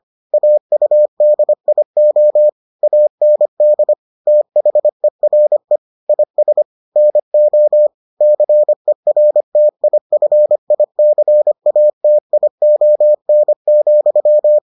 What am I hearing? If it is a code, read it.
Morse code, heard as a keyed tone, about 25 wpm, 600 Hz. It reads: AUDIO AND THERE IS NO CERTIFICATION,